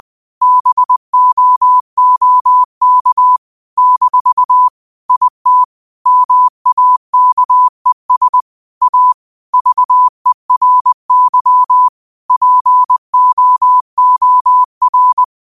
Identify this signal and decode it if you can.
Morse code, heard as a keyed tone, about 20 wpm, 1000 Hz. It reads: BOOK - IT MAKES A VERY POOR
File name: 20wpm